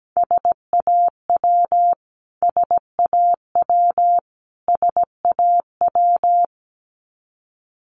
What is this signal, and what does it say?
Morse code, heard as a keyed tone, about 17 wpm, 700 Hz. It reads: SAW SAW SAW
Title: Top 500 Words Repeated 17wpm